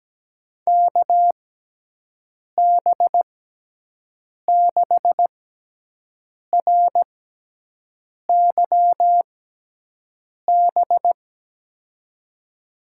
Morse code, heard as a keyed tone, about 17 wpm, spaced as for 6 wpm, 700 Hz. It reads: KB6RYB
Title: USA Call Signs - 6 Characters - 6x Character Spacing - 17wpm